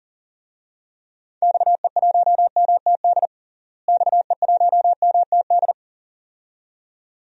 Morse code, heard as a keyed tone, about 40 wpm, 700 Hz. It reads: XE1MTD XE1MTD
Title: DXCC-100 Call Signs - 6 Characters - Repeated 2x - 40wpm